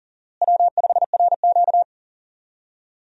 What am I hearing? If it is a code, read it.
Morse code, heard as a keyed tone, about 40 wpm, 700 Hz. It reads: W5RQ
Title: USA Call Signs - 4 Characters - Repeated 2x - 40wpm